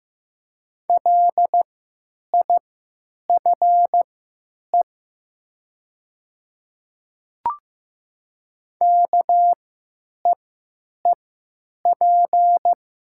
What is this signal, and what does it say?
Morse code, heard as a keyed tone, about 15 wpm, spaced as for 9 wpm, 700 Hz. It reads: LIFE KEEP
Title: 3x Character Spacing - Top 300 Words - 15wpm